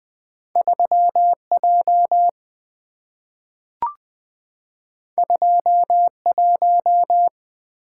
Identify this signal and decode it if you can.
Morse code, heard as a keyed tone, about 20 wpm, 700 Hz. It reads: 3J 21
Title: Two Letter-Number Combinations - 20wpm